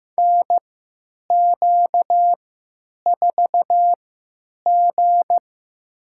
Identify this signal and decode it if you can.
Morse code, heard as a keyed tone, about 15 wpm, spaced as for 9 wpm, 700 Hz. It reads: NQ4G
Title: USA Call Signs - 4 Characters - 3x Character Spacing - 15wpm